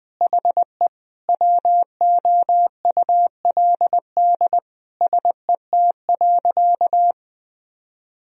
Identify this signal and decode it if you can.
Morse code, heard as a keyed tone, about 20 wpm, 700 Hz. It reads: HE WOULD SET.
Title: Sentences from Top 1000 Words - Review Part 1 - 20wpm